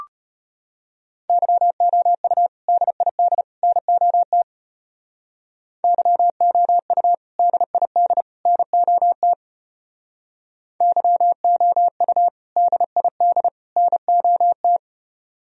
Morse code, heard as a keyed tone, about 34 wpm, 700 Hz. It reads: YOU DID NOT YOU DID NOT YOU DID NOT